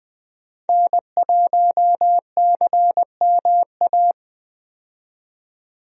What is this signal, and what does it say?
Morse code, heard as a keyed tone, about 20 wpm, 700 Hz. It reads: N1CMA
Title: USA Call Signs - 5 Characters - 20wpm